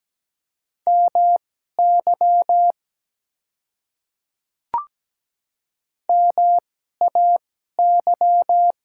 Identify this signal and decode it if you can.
Morse code, heard as a keyed tone, about 17 wpm, spaced as for 12 wpm, 700 Hz. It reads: MY MAY
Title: 2x Character Spacing - Top 100 Words - 17wpm